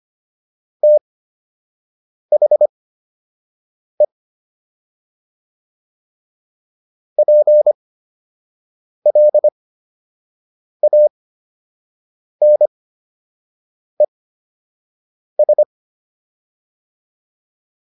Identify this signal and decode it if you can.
Morse code, heard as a keyed tone, about 25 wpm, spaced as for 6 wpm, 600 Hz. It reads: THE PLANES